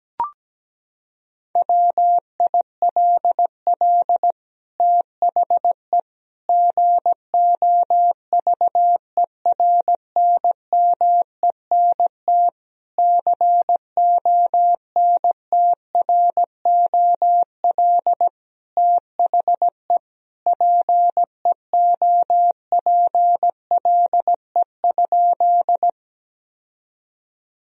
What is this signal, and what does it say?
Morse code, heard as a keyed tone, about 17 wpm, 700 Hz. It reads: WILL THE GOVERNMENT CONTROL THE PEOPLE?
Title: Sentences from Top 1000 Words - Review Part 3 - 17wpm